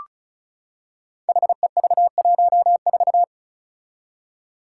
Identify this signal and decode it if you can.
Morse code, heard as a keyed tone, about 36 wpm, 700 Hz. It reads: HEV14
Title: Five Challenging Letter-Number Combinations 35wpm